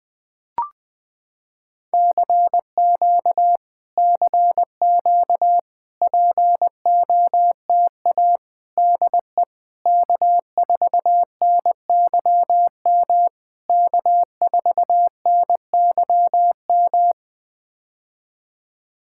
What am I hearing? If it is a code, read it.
Morse code, heard as a keyed tone, about 20 wpm, 700 Hz. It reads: CQ CQ POTA DE K4NYM K4NYM
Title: POTA QSO Practice - 20wpm